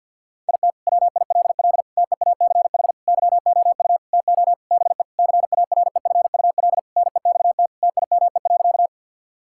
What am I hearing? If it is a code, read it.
Morse code, heard as a keyed tone, about 50 wpm, 700 Hz. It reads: IT WILL TEACH YOU TO BE CAREFUL NEXT TIME.
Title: Sentences from Top 1600 Words 50wpm